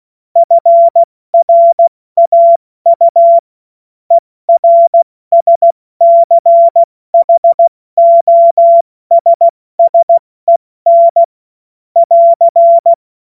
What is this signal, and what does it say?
Morse code, heard as a keyed tone, about 16 wpm, spaced as for 14 wpm, 680 Hz. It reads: FRAU ERSCHOSSEN <AR>
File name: Morsekurs / Witze / Witz 07. März 2023 70 F80